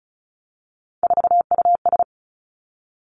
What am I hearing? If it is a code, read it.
Morse code, heard as a keyed tone, about 36 wpm, 700 Hz. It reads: 4US
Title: Three Challenging Letter-Number Combinations 35wpm